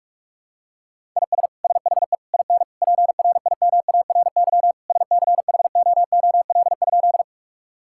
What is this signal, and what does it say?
Morse code, heard as a keyed tone, about 46 wpm, 700 Hz. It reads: IS SHE IN PRIMARY SCHOOL?
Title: Sentences from Top 1000 Words - Review Part 5 - 45wpm